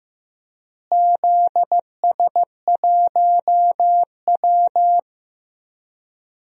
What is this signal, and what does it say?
Morse code, heard as a keyed tone, about 15 wpm, 700 Hz. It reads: ZS1W
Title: DXCC-100 Call Signs - 4 Characters - Repeated 2x - 15wpm